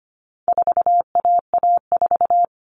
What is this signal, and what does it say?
Morse code, heard as a keyed tone, about 25 wpm, 700 Hz. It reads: 4AA4
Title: All Cut Numbers - 4-Digit 25wpm